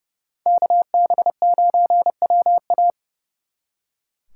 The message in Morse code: KB9WA

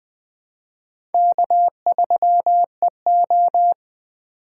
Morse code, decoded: K3EO

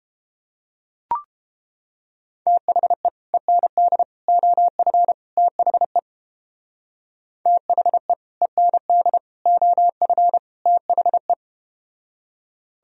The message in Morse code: THE END OF THE THE END OF THE

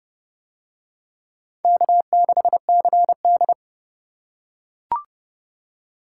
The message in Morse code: K6CD

